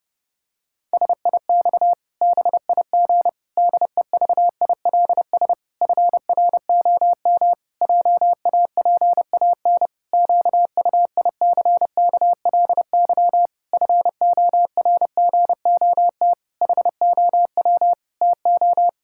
Morse code, heard as a keyed tone, about 32 wpm, 700 Hz. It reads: SIX BIG DEVILS FROM JAPAN QUICKLY FORGOT HOW TO
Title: Pangram Challenge v2 - 30wpm